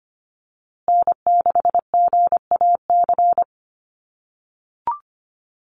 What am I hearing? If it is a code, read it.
Morse code, heard as a keyed tone, about 25 wpm, 700 Hz. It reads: N6GAC
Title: USA Call Signs Repeated 3x 25wpm